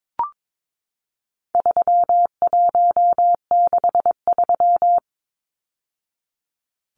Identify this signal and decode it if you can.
Morse code, heard as a keyed tone, about 22 wpm, 700 Hz. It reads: 3163
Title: Numbers - 4-Digit 22wpm